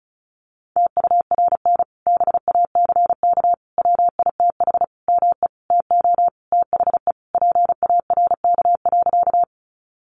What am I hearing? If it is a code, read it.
Morse code, heard as a keyed tone, about 34 wpm, 700 Hz. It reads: TURN BACK WITH ME TO THE PARK.
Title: Sentences from Top 1300 Words 35wpm